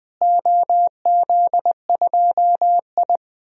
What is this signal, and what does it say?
Morse code, heard as a keyed tone, about 19 wpm, 700 Hz. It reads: OZ2I
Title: DXCC-100 Call Signs - 4 Characters - Repeated 3x - 20wpm